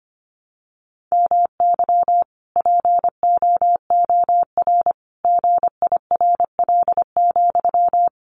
Morse code, heard as a keyed tone, about 25 wpm, 700 Hz. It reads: MY POOR GIRL,
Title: Sentences from Top 1000 Words 25wpm